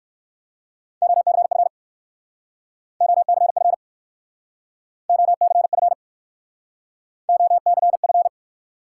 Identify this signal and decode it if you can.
Morse code, heard as a keyed tone, about 57 wpm, 700 Hz. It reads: OCF OCF OCF OCF